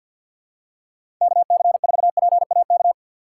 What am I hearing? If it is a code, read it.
Morse code, heard as a keyed tone, about 52 wpm, 700 Hz. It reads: KC4PAK
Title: USA Call Signs - 6 Characters - 6x Character Spacing - 50wpm